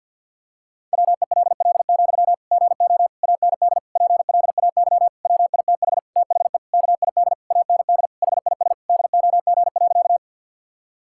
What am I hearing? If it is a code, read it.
Morse code, heard as a keyed tone, about 50 wpm, 700 Hz. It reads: WELL, GO AND PLAY WITH THE KID AND HIS DOG.
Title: Sentences from Top 1600 Words 50wpm